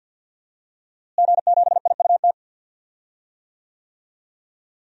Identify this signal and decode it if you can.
Morse code, heard as a keyed tone, about 49 wpm, 700 Hz. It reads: G7IUT
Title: DXCC-100 Call Signs - 5 Characters - Repeated 2x - 50wpm